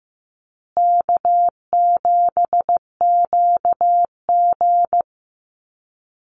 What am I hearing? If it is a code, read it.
Morse code, heard as a keyed tone, about 15 wpm, 700 Hz. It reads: K7QG
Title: USA Call Signs Repeated 3x 15wpm